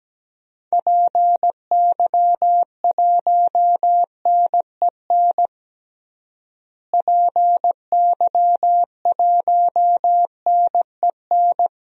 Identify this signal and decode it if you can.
Morse code, heard as a keyed tone, about 17 wpm, 700 Hz. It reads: PY1NEN PY1NEN
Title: DXCC-100 Call Signs - 6 Characters - Repeated 3x - 17wpm